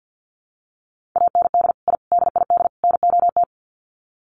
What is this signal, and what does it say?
Morse code, heard as a keyed tone, about 50 wpm, 700 Hz. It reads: AND I DID NOT
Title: Sets of 4 Words 50wpm